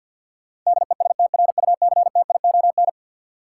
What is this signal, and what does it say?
Morse code, heard as a keyed tone, about 52 wpm, 700 Hz. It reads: DESTRUCTION
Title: Sets of 5 Words Encore - New Word Drill - 50wpm